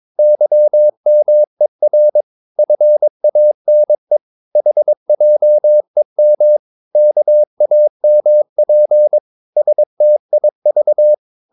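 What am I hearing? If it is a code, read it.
Morse code, heard as a keyed tone, about 22 wpm, 600 Hz. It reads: YMER FANE HJEM KAMP STIV
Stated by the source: Morse practice of short Danish words at 22 wpm, from CW med Gnister.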